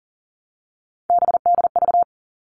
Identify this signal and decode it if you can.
Morse code, heard as a keyed tone, about 40 wpm, 700 Hz. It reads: BDV
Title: All Cut Numbers - 3-Digit 40wpm